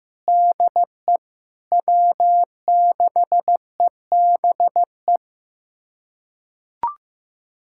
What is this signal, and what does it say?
Morse code, heard as a keyed tone, about 15 wpm, 700 Hz. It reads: DE W6EBE
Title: USA Callsign DE Callsign - 5 Characters - 15wpm